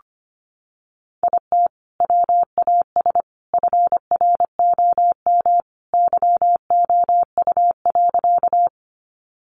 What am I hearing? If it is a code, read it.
Morse code, heard as a keyed tone, about 25 wpm, 700 Hz. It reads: IT WAS FROM YOU.
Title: Sentences from Top 100 Words 25wpm